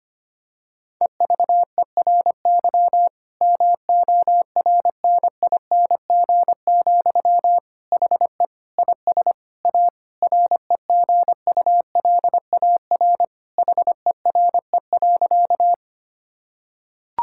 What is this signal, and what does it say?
Morse code, heard as a keyed tone, about 25 wpm, 700 Hz. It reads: EVERY MORNING, HE IS A REGULAR HERE.
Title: Sentences from Top 1000 Words - Review Part 6 - 25wpm